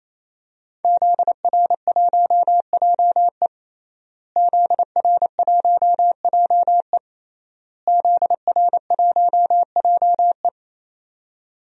Morse code, heard as a keyed tone, about 28 wpm, 700 Hz. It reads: ZR1JE ZR1JE ZR1JE
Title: DXCC-100 Call Signs - 5 Characters - Repeated 3x - 28wpm